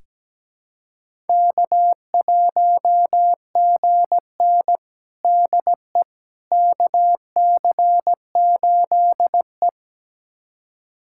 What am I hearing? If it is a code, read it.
Morse code, heard as a keyed tone, about 17 wpm, 700 Hz. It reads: K1GN DE KC8E
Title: USA Callsign DE Callsign - 4 Characters - 17wpm